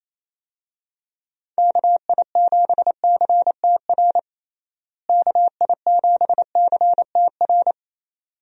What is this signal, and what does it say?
Morse code, heard as a keyed tone, about 28 wpm, 700 Hz. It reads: KI7CTR KI7CTR
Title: USA Call Signs - 6 Characters - Repeated 2x - 28wpm